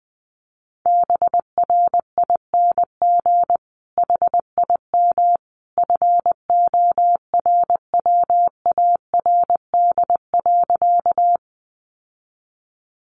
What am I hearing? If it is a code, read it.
Morse code, heard as a keyed tone, about 20 wpm, 700 Hz. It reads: BRING HIM FORWARD.
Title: Sentences from Top 1300 Words 20wpm